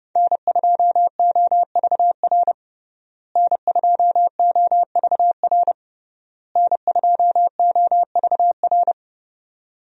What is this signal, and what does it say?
Morse code, heard as a keyed tone, about 30 wpm, 700 Hz. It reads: N2OVR N2OVR N2OVR
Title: USA Call Signs - 5 Characters - Repeated 3x - 30wpm